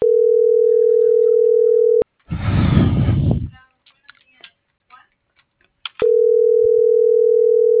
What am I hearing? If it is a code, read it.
Ringback tone (North America), 440 + 480 Hz.